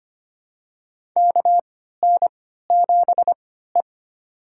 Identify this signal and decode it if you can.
Morse code, heard as a keyed tone, about 25 wpm, spaced as for 14 wpm, 700 Hz. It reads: KN7E
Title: USA Call Signs - 4 Characters - 3x Character Spacing - 25wpm